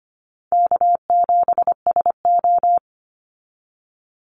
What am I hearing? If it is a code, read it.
Morse code, heard as a keyed tone, about 25 wpm, 690 Hz. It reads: K7SO
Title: NA SOTA Activator Callsigns 25wpm